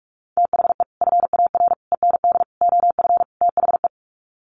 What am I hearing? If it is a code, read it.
Morse code, heard as a keyed tone, about 44 wpm, 700 Hz. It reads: THE FAR END OF THE
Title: Sets of 5 Words 45wpm